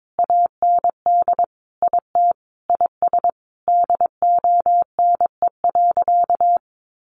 Morse code, heard as a keyed tone, about 22 wpm, 700 Hz. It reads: AND IT IS DONE.
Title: Sentences from Top 1500 Words 22wpm